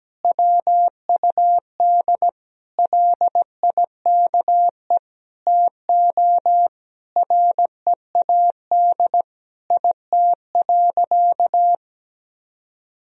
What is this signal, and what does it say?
Morse code, heard as a keyed tone, about 17 wpm, 700 Hz. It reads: WUD LIKE TO READ IT.
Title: Sentences from Top 500 Words and Abbreviations 17wpm